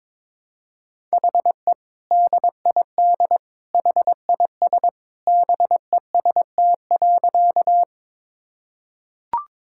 Morse code, heard as a keyed tone, about 22 wpm, 700 Hz. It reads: HE DID HIS BEST.
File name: practice / 22wpm / Sentences with 2-Word Phrases - 22wpm